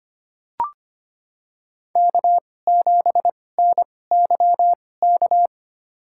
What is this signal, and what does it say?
Morse code, heard as a keyed tone, about 25 wpm, spaced as for 18 wpm, 700 Hz. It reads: K7NYK
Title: USA Call Signs - 5 Characters - 2x Character Spacing - 25wpm